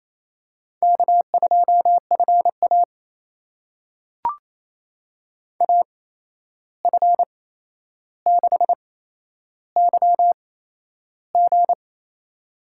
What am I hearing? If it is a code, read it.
Morse code, heard as a keyed tone, about 28 wpm, spaced as for 11 wpm, 700 Hz. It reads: K2FA AF6YG
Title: USA Call Signs - 8x Character Spacing - 28wpm